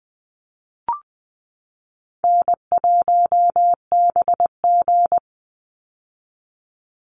Morse code, heard as a keyed tone, about 20 wpm, 700 Hz. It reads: N1BG
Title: Easy Call Signs - 20wpm